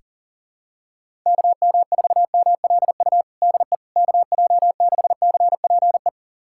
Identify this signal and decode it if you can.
Morse code, heard as a keyed tone, about 37 wpm, 700 Hz. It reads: KM4MLU DE KJ6CPE
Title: USA Callsign DE Callsign 40wpm